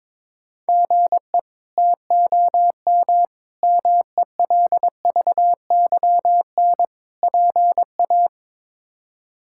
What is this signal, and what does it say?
Morse code, heard as a keyed tone, about 22 wpm, 700 Hz. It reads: GE TOM MELVYN PA
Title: K1USN SST QSO Practice - 22wpm